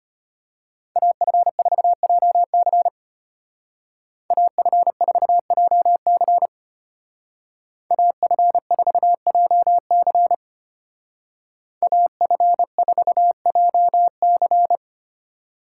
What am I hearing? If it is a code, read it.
Morse code, heard as a keyed tone, about 33 wpm, 700 Hz. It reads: AF4JC AF4JC AF4JC AF4JC